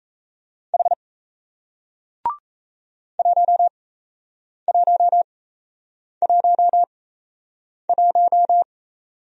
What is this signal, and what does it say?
Morse code, heard as a keyed tone, about 37 wpm, spaced as for 8 wpm, 700 Hz. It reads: H 1111